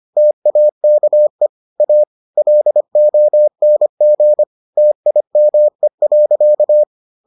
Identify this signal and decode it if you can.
Morse code, heard as a keyed tone, about 25 wpm, 600 Hz. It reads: TAKE A LONG TIME.